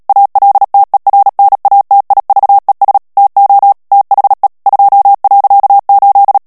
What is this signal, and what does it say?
Morse code, heard as a keyed tone, about 37 wpm, 800 Hz. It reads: ALTERNATIVES TO THE 2.8
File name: morse mp3 / ANGLAIS texte clair 3 vitesse 35 mots minute